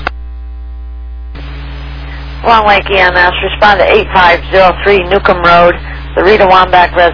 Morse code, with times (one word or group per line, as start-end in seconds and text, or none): none